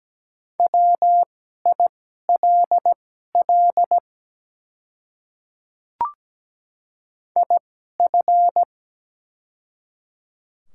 Morse code, heard as a keyed tone, about 17 wpm, spaced as for 12 wpm, 700 Hz. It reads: WILL IF
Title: 2x Character Spacing - Top 200 Words - 17wpm